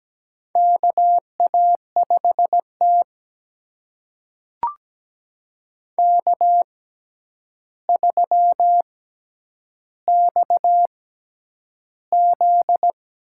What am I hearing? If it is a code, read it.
Morse code, heard as a keyed tone, about 17 wpm, spaced as for 9 wpm, 700 Hz. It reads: KA5T K3XZ
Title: USA Call Signs - 4 Characters - 6x Character Spacing - 17wpm